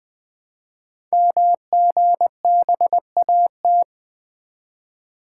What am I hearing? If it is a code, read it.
Morse code, heard as a keyed tone, about 20 wpm, 700 Hz. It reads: MGBAT
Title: Code Groups - 5 Letter:Numbers - 20wpm